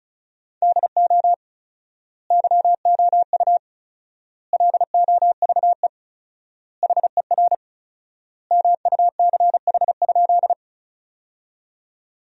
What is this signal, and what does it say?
Morse code, heard as a keyed tone, about 36 wpm, 700 Hz. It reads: DO YOU LOVE HER MUCH?
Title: 4x Word-Spacing - Sentences from Top 400 Words - 35wpm